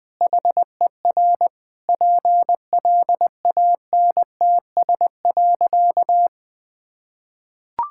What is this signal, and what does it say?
Morse code, heard as a keyed tone, about 20 wpm, 700 Hz. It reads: HER PLANTS.
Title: Sentences with 4-Word Phrases - 20wpm